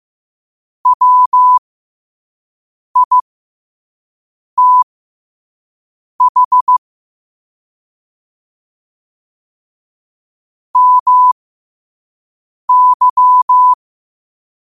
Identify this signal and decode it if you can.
Morse code, heard as a keyed tone, about 15 wpm, spaced as for 5 wpm, 1000 Hz. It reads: WITH MY
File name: qotdfiles / 5wpm